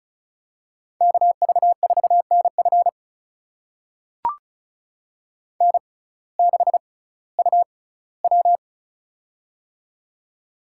Morse code, heard as a keyed tone, about 36 wpm, spaced as for 19 wpm, 700 Hz. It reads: KV4NF N6UW
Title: USA Call Signs - 6x Character Spacing - 35wpm